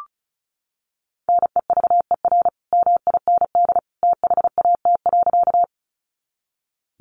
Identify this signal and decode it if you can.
Morse code, heard as a keyed tone, about 34 wpm, 700 Hz. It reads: NEVER MIND THAT.
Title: Sentences from Top 500 Words 35wpm